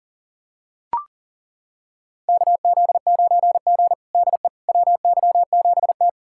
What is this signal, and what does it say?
Morse code, heard as a keyed tone, about 40 wpm, 700 Hz. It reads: KZ9G DE WY7T
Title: USA Callsign DE Callsign - 4 Characters - 40wpm